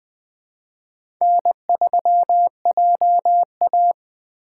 Morse code, heard as a keyed tone, about 20 wpm, 700 Hz. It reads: N3JA
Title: Speed Racing - USA 1x2 Call Signs 17wpm